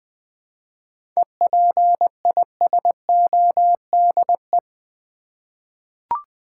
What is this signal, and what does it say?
Morse code, heard as a keyed tone, about 20 wpm, 700 Hz. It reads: EPISODE
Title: 7 Letter Words - First 3 Letters Given - 20wpm